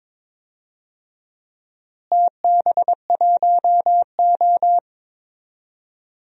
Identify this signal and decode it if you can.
Morse code, heard as a keyed tone, about 22 wpm, 700 Hz. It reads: TB1O
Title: DXCC-100 Call Signs - 4 Characters - Repeated 3x - 22wpm